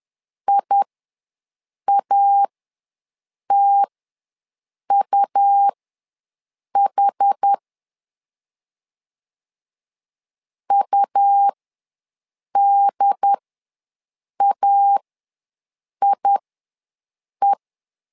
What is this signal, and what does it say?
Morse code, heard as a keyed tone, about 10 wpm, spaced as for 6 wpm, 800 Hz. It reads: IATUH UDAIE
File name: LEZIONI / 2 UHDN / 2 UHD ws 10wpm 2058